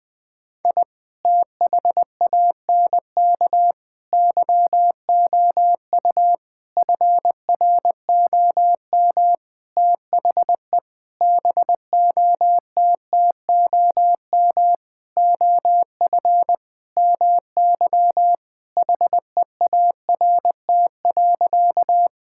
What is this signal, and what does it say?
Morse code, heard as a keyed tone, about 21 wpm, 700 Hz. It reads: I THANK YOU FROM THE BOTTOM OF MY HEART.
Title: Sentences with 8-Word Phrases - 20wpm